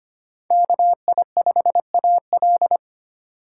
Morse code, heard as a keyed tone, about 25 wpm, 700 Hz. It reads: KI5AL
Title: K1USN SST QSO Practice - 25wpm